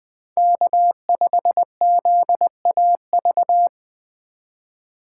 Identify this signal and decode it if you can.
Morse code, heard as a keyed tone, about 21 wpm, 700 Hz. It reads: K5ZAV
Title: USA Call Signs - 5 Characters - 3x Character Spacing - 20wpm